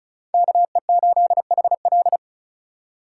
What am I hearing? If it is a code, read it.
Morse code, heard as a keyed tone, about 46 wpm, 700 Hz. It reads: KE8HL